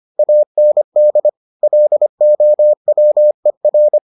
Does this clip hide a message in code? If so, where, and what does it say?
Yes: Morse code from 0.2 to 4.0 s, heard as a keyed tone, about 25 wpm, 600 Hz. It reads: AND LOWER